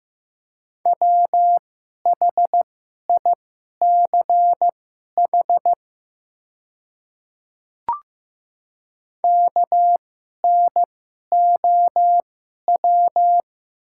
Morse code, heard as a keyed tone, about 15 wpm, spaced as for 11 wpm, 700 Hz. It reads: WHICH KNOW
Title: 2x Character Spacing - Top 200 Words - 15wpm